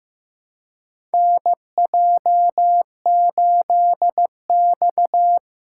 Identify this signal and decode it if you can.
Morse code, heard as a keyed tone, about 15 wpm, 700 Hz. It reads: NJ8X
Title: USA Call Signs - 4 Characters - 3x Character Spacing - 15wpm